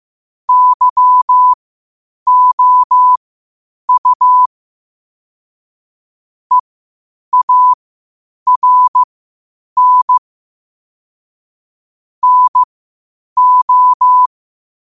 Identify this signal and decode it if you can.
Morse code, heard as a keyed tone, about 15 wpm, spaced as for 8 wpm, 1000 Hz. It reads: YOU EARN NO
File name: qotdfiles / 8wpm